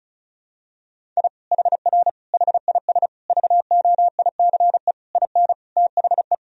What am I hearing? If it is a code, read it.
Morse code, heard as a keyed tone, about 36 wpm, 700 Hz. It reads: I HR HIS VOICE IN THE
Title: Sentences from Top 500 Words and Abbreviations 35wpm